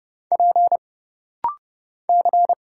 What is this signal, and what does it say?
Morse code, heard as a keyed tone, about 30 wpm, 700 Hz. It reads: P C
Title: Single Letter-Number Alt - Rapid-Fire 30wpm